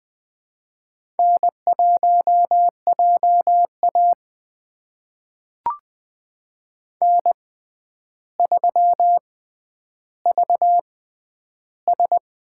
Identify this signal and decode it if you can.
Morse code, heard as a keyed tone, about 20 wpm, spaced as for 10 wpm, 700 Hz. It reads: N1JA N3VS
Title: USA Call Signs - 4 Characters - 6x Character Spacing - 20wpm